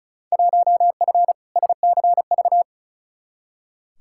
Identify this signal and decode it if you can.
Morse code, heard as a keyed tone, about 36 wpm, 700 Hz. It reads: 1F SCV
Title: Field Day Exchanges Encore - 35wpm